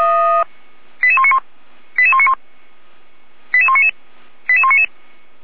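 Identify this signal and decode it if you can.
Five-tone selective call (ZVEI), heard as quick runs of five tones, tones of about 70 ms. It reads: 88191 88191 88199 88199
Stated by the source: ZVEI five-tone selective call for 88288, received by SDR.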